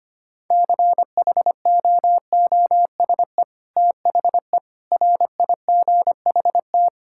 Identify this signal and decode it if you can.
Morse code, heard as a keyed tone, about 25 wpm, 700 Hz. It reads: CHOOSE THE RIGHT
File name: practice / 25wpm / Sentences with 2-Word Phrases - 25wpm